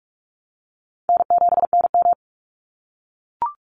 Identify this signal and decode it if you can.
Morse code, heard as a keyed tone, about 44 wpm, 700 Hz. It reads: N7NM
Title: Easy Call Signs - 45wpm